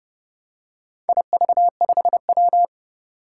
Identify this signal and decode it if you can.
Morse code, heard as a keyed tone, about 28 wpm, 700 Hz. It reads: IV5W
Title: Four Challenging Letter-Number Combinations 30wpm